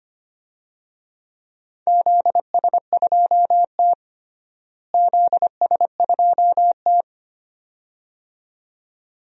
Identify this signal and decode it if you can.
Morse code, heard as a keyed tone, about 25 wpm, 700 Hz. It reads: ZS2T ZS2T
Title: DXCC-100 Call Signs - 4 Characters - Repeated 2x - 25wpm